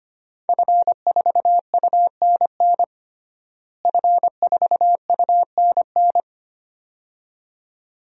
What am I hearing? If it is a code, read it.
Morse code, heard as a keyed tone, about 25 wpm, 700 Hz. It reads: F4UNN F4UNN
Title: DXCC-100 Call Signs - 5 Characters - Repeated 2x - 25wpm